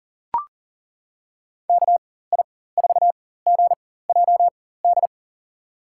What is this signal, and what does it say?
Morse code, heard as a keyed tone, about 40 wpm, spaced as for 19 wpm, 700 Hz. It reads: KI4GJD
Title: USA Call Signs - 6 Characters - 4x Character Spacing - 40wpm